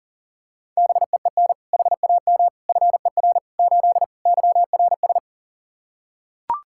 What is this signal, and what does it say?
Morse code, heard as a keyed tone, about 40 wpm, 700 Hz. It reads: BEEN HAM FER 8 YRS
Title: Practice QSO - Exchange 3 - More Info - 40wpm